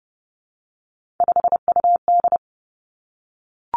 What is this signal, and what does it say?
Morse code, heard as a keyed tone, about 30 wpm, 700 Hz. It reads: 5UD
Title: Three Challenging Letter-Number Combinations 30wpm